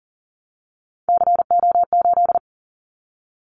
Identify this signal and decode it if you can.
Morse code, heard as a keyed tone, about 40 wpm, 700 Hz. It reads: CO8
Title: Callsign Prefixes for Top 100 DXCC - 40wpm